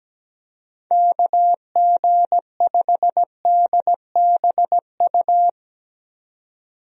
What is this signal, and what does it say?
Morse code, heard as a keyed tone, about 17 wpm, 700 Hz. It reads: KG5DBU
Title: USA Call Signs - 6 Characters - 17wpm